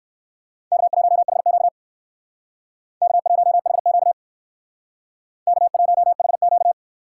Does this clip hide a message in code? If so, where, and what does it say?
Yes: Morse code from 0.7 to 6.7 s, heard as a keyed tone, about 58 wpm, 700 Hz. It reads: K1HQ K1HQ K1HQ